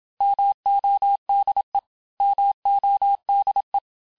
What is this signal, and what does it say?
Morse code, heard as a keyed tone, about 26 wpm, 800 Hz. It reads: MODE MODE
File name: cours-du-mercredi / audios / cours-869-texte-LES-MOTS-vitesse-25-MPM